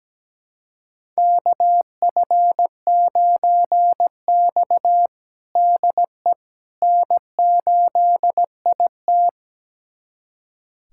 Morse code, heard as a keyed tone, about 17 wpm, 700 Hz. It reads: KF9X DE N8IT
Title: USA Callsign DE Callsign - 4 Characters - 17wpm